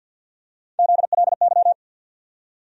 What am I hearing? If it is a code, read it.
Morse code, heard as a keyed tone, about 48 wpm, 700 Hz. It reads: ZLY
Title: USA Call Sign Postfixes - 3 Letters 50wpm